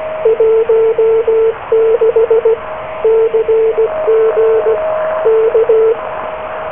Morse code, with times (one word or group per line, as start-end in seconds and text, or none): 0.3-2.5 16
3.0-4.8 CG
5.3-5.9 K